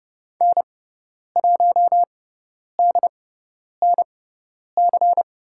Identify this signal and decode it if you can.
Morse code, heard as a keyed tone, about 30 wpm, spaced as for 10 wpm, 700 Hz. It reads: N1DNC
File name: Lesson 16g - Letter c - Call Signs 30wpm